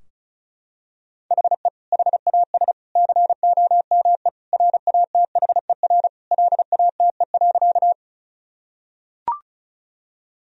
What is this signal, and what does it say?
Morse code, heard as a keyed tone, about 36 wpm, 700 Hz. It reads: HE HAS COME RATHER LATE.
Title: Sentences from Top 1000 Words - Review Part 4 - 35wpm